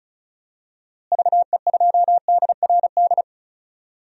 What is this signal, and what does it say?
Morse code, heard as a keyed tone, about 34 wpm, 700 Hz. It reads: VE2DRD
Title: DXCC-100 Call Signs - 6 Characters - Repeated 3x - 35wpm